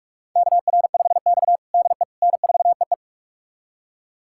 Morse code, heard as a keyed tone, about 42 wpm, 700 Hz. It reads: KR5X DE N4EE
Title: USA Callsign DE Callsign - 4 Characters - 45wpm